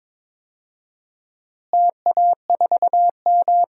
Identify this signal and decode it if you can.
Morse code, heard as a keyed tone, about 22 wpm, 700 Hz. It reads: TA4M
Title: DXCC-100 Call Signs - 4 Characters - Repeated 3x - 22wpm